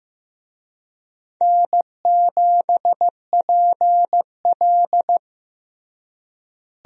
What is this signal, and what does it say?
Morse code, heard as a keyed tone, about 15 wpm, 700 Hz. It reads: N7PL
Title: USA Call Signs - 4 Characters - Repeated 3x - 15wpm